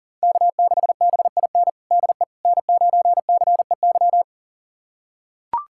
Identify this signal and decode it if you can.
Morse code, heard as a keyed tone, about 40 wpm, 700 Hz. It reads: K6BIN DE N9CEY
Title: USA Callsign DE Callsign - 5 Characters - 40wpm